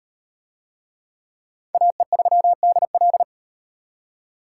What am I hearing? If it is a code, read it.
Morse code, heard as a keyed tone, about 34 wpm, 700 Hz. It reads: AE3DL